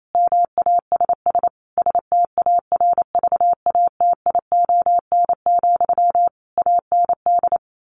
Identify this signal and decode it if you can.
Morse code, heard as a keyed tone, about 28 wpm, 700 Hz. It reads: MASS STARVATION, AND
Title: A Sound Of Thunder 28wpm